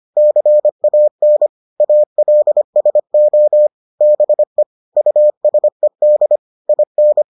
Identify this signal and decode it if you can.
Morse code, heard as a keyed tone, about 25 wpm, 600 Hz. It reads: CAN ALSO BE USED IN